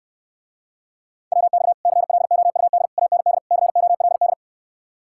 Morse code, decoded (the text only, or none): WX CLOUD AND COLD